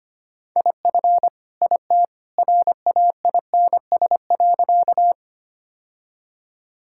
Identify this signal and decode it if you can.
Morse code, heard as a keyed tone, about 25 wpm, 700 Hz. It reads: IF IT RAINS.
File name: practice / 25wpm / Sentences with 2-Word Phrases - 25wpm